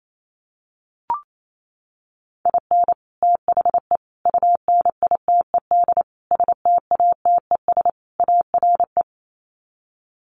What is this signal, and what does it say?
Morse code, heard as a keyed tone, about 28 wpm, 700 Hz. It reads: IN THE UNITED STATES ARE
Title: Sets of 5 Words Encore 28wpm